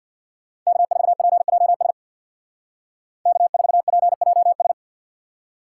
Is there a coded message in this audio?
Yes: Morse code from 0.7 to 4.7 s, heard as a keyed tone, about 64 wpm, 700 Hz. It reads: K4PJS K4PJS